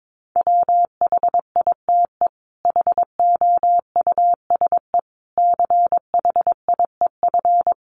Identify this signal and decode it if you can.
Morse code, heard as a keyed tone, about 22 wpm, 700 Hz. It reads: WHITE HOUSE CHIEF
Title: Sets of 5 Words Encore 22wpm